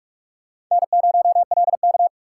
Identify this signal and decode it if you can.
Morse code, heard as a keyed tone, about 46 wpm, 700 Hz. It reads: N0LK
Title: USA Call Signs - 4x Character Spacing - 45wpm